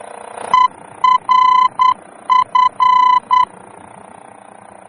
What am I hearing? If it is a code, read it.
Morse code, heard as a keyed tone, about 9 wpm, 1010 Hz. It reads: ERF
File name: germany / mx / ERF-D-VOR